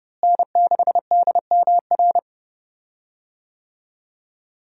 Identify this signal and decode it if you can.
Morse code, heard as a keyed tone, about 32 wpm, 700 Hz. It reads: N6DMR
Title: USA Call Signs - 5 Characters - Repeated 3x - 30wpm